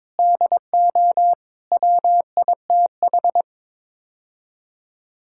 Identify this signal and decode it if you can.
Morse code, heard as a keyed tone, about 21 wpm, 700 Hz. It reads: DO WITH
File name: practice / 22wpm / Build A Phrase - 5-Word Phrases - 22wpm